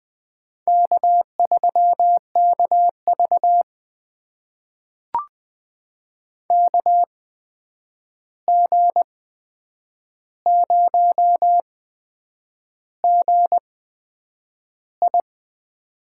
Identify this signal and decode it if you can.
Morse code, heard as a keyed tone, about 20 wpm, spaced as for 8 wpm, 700 Hz. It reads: K3KV KG0GI
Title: USA Call Signs - 8x Character Spacing - 20wpm